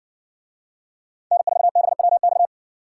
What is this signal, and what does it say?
Morse code, heard as a keyed tone, about 61 wpm, 690 Hz. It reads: N4BWX